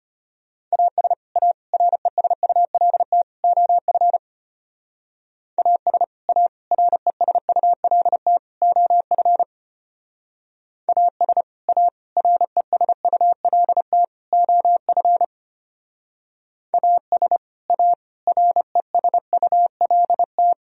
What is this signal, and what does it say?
Morse code, heard as a keyed tone, about 30 wpm, 700 Hz. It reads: AS A RESULT OF AS A RESULT OF AS A RESULT OF AS A RESULT
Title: Speed-Racing - Sets of 4 Words 25wpm